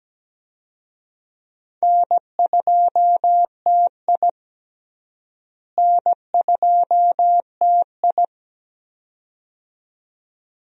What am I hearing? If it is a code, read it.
Morse code, heard as a keyed tone, about 17 wpm, 700 Hz. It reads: N2TI N2TI
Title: USA Call Signs - 4 Characters - Repeated 2x - 17wpm